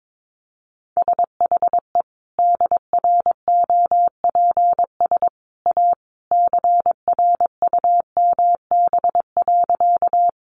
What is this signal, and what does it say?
Morse code, heard as a keyed tone, about 22 wpm, 700 Hz. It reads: SHE DROPS A CRUMB.
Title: Easy Phrases 22wpm